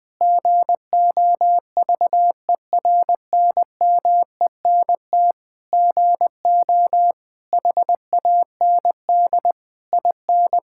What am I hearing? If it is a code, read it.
Morse code, heard as a keyed tone, about 20 wpm, 700 Hz. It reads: GOVERNMENT GO HAND IN
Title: Sentences from Top 1000 Words - Review Part 5 - 20wpm